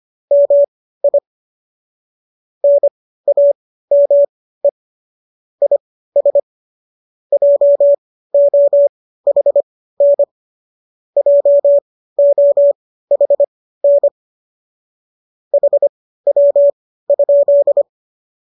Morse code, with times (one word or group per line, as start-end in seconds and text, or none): 0.3-1.2 MI
2.6-4.7 NAME
5.6-6.4 IS
7.3-10.2 JOHN
11.2-14.1 JOHN
15.5-17.8 HW?